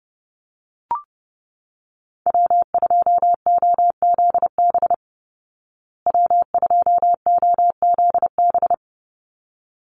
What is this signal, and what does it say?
Morse code, heard as a keyed tone, about 30 wpm, 700 Hz. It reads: W2OZB W2OZB
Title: Call Signs Repeated 3x - 30wpm